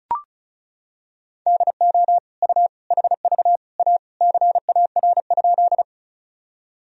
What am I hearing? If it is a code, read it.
Morse code, heard as a keyed tone, about 34 wpm, 700 Hz. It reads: DO U HV A CAR?
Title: Sentences from Top 300 Words and Abbreviations 35wpm